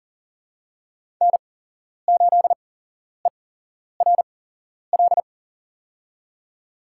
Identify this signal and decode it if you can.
Morse code, heard as a keyed tone, about 40 wpm, spaced as for 11 wpm, 700 Hz. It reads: N8ERL
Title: USA Call Signs - 5 Characters - 8x Character Spacing - 40wpm